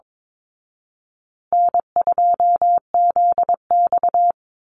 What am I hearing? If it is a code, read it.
Morse code, heard as a keyed tone, about 22 wpm, 700 Hz. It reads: N2ZX
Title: Easy Call Signs Repeated 3x - 22wpm